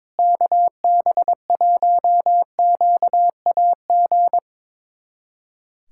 Morse code, heard as a keyed tone, about 22 wpm, 700 Hz. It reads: KB1QAG
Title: USA Callsign DE Callsign 22wpm